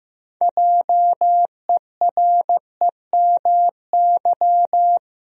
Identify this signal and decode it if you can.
Morse code, heard as a keyed tone, about 16 wpm, 700 Hz. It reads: JEREMY
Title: USA Common Names - 15wpm